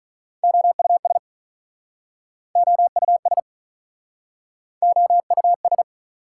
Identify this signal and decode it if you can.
Morse code, heard as a keyed tone, about 48 wpm, 700 Hz. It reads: OUS OUS OUS